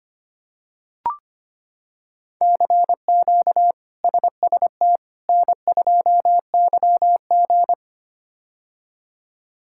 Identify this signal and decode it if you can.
Morse code, heard as a keyed tone, about 25 wpm, 700 Hz. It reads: CQ SST N2YG
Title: K1USN SST QSO Practice - 25wpm